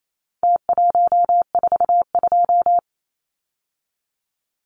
Morse code, heard as a keyed tone, about 28 wpm, 700 Hz. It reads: T142
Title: Numbers - Contest Style - 4-Digit 28wpm